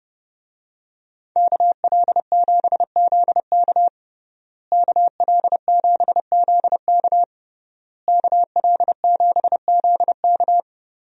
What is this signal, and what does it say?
Morse code, heard as a keyed tone, about 29 wpm, 700 Hz. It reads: KL7ZK KL7ZK KL7ZK
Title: USA Call Signs - 5 Characters - Repeated 3x - 30wpm